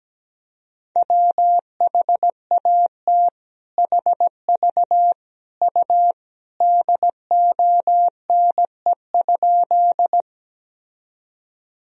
Morse code, heard as a keyed tone, about 17 wpm, 700 Hz. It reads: WHAT HV U DONE?
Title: Sentences from Top 500 Words and Abbreviations 17wpm